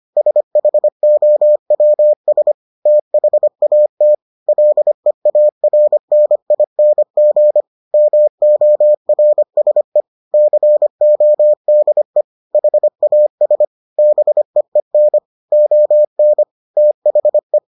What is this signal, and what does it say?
Morse code, heard as a keyed tone, about 25 wpm, 600 Hz. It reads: SHOWS THAT LEARNING MORSE CODE HAS BEEN ON THE